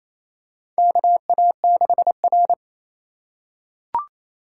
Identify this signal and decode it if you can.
Morse code, heard as a keyed tone, about 27 wpm, 700 Hz. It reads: KA6R
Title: USA Call Signs - 3x Character Spacing - 28wpm